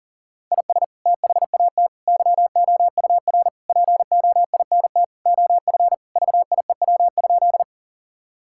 Morse code, heard as a keyed tone, about 40 wpm, 700 Hz. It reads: IS THAT YOUR POINT OF VIEW?
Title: Sentences from Top 1000 Words - Review Part 6 - 40wpm